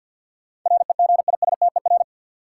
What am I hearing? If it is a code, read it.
Morse code, heard as a keyed tone, about 53 wpm, 700 Hz. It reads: REGISTER
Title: Sentences from Top 1200 Words - New Word Drill Alt - 50wpm